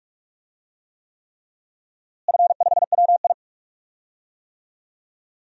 Morse code, heard as a keyed tone, about 42 wpm, 700 Hz. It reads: F5WI
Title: DXCC-100 Call Signs - 4 Characters - Repeated 3x - 45wpm